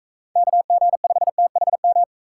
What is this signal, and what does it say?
Morse code, heard as a keyed tone, about 38 wpm, 700 Hz. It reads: KG5THM